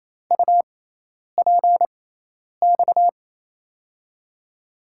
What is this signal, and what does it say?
Morse code, heard as a keyed tone, about 28 wpm, spaced as for 10 wpm, 700 Hz. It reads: UPX
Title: USA Call Signs - 6x Character Spacing - 28wpm